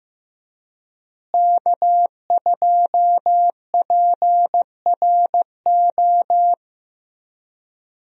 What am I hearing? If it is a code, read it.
Morse code, heard as a keyed tone, about 15 wpm, 700 Hz. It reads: K2PRO
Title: USA Call Signs - 5 Characters - Repeated 3x - 15wpm